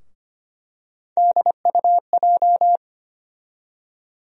Morse code, heard as a keyed tone, about 25 wpm, 700 Hz. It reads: DUJ
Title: USA Call Sign Postfixes - 3 Letters 25wpm